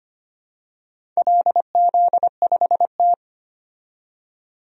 Morse code, heard as a keyed tone, about 25 wpm, 700 Hz. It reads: LZ5T
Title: DXCC-100 Call Signs - 4 Characters - 25wpm